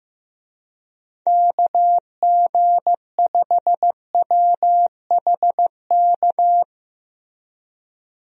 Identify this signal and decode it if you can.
Morse code, heard as a keyed tone, about 15 wpm, 700 Hz. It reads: KG5WHK
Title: USA Call Signs - 6 Characters - Repeated 2x - 15wpm